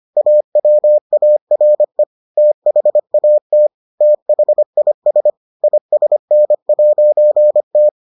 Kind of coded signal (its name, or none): Morse code